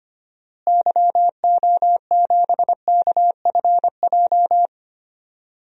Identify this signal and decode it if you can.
Morse code, heard as a keyed tone, about 25 wpm, 700 Hz. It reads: YO7KFJ
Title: DXCC-100 Call Signs - 6 Characters - Repeated 2x - 25wpm